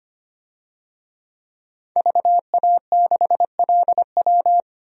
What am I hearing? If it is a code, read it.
Morse code, heard as a keyed tone, about 25 wpm, 700 Hz. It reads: VA6LW
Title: DXCC-100 Call Signs - 5 Characters - 25wpm